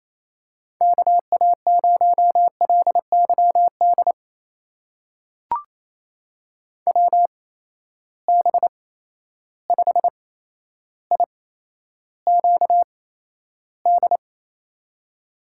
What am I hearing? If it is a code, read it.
Morse code, heard as a keyed tone, about 27 wpm, spaced as for 12 wpm, 700 Hz. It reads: KA0LYD WB5IQD
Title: USA Call Signs - 6 Characters - 8x Character Spacing - 28wpm